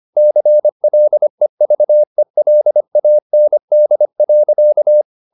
Morse code, heard as a keyed tone, about 25 wpm, 600 Hz. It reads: CLEVELAND.
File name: the old man and the sea part 1 25wpm